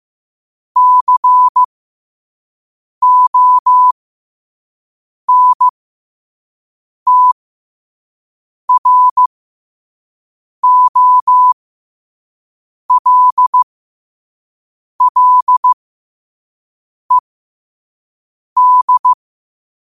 Morse code, heard as a keyed tone, about 15 wpm, spaced as for 5 wpm, 1000 Hz. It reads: CONTROLLED
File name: qotdfiles / 5wpm